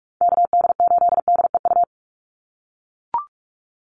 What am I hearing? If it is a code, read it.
Morse code, heard as a keyed tone, about 44 wpm, 700 Hz. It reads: KD8DEU
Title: NA SOTA Chaser Callsigns 45wpm